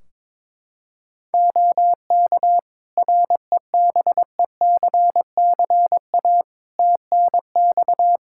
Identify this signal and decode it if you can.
Morse code, heard as a keyed tone, about 22 wpm, 700 Hz. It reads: OK REBECCA TNX
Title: Practice QSO - Exchange 4 - Ending - 22wpm